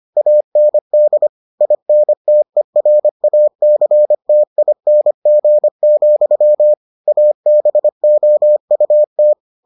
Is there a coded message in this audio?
Yes: Morse code from 0.2 to 9.3 s, heard as a keyed tone, about 25 wpm, 600 Hz. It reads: AND INTERACTING, ABOUT